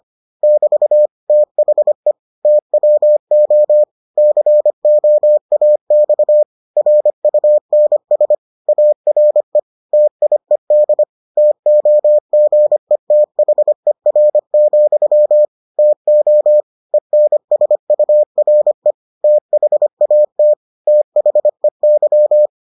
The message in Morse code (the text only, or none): <BT> THE TWO COAX RUNS ARE TIED TOGETHER, TO ENSURE THAT THEY